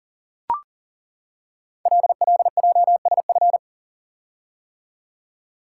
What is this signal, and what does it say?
Morse code, heard as a keyed tone, about 40 wpm, 700 Hz. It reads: LLJSF
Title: Code Groups - 5 Letters - 40wpm